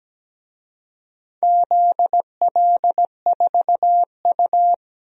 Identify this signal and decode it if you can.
Morse code, heard as a keyed tone, about 17 wpm, 700 Hz. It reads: ZL4U
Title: DXCC-100 Call Signs - 4 Characters - 17wpm